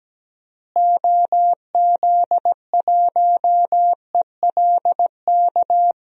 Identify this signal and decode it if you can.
Morse code, heard as a keyed tone, about 17 wpm, 700 Hz. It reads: OZ1ELK
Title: DXCC-100 Call Signs - 6 Characters - Repeated 2x - 17wpm